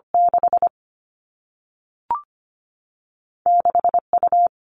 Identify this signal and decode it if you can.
Morse code, heard as a keyed tone, about 25 wpm, 700 Hz. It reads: 6 6 U
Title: All Cut Numbers - 2-Digit 25wpm